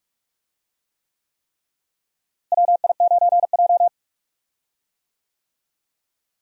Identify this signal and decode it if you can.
Morse code, heard as a keyed tone, about 44 wpm, 700 Hz. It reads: WI9J
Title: USA Call Signs - 4 Characters - Repeated 3x - 45wpm